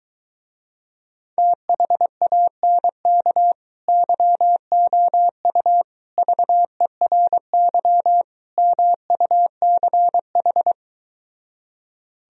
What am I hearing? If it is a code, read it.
Morse code, heard as a keyed tone, about 22 wpm, 700 Hz. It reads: THANK YOU VERY MUCH